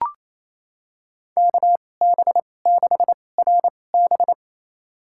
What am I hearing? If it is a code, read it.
Morse code, heard as a keyed tone, about 28 wpm, spaced as for 20 wpm, 700 Hz. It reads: KB6RB
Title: USA Call Signs - 5 Characters - 2x Character Spacing - 28wpm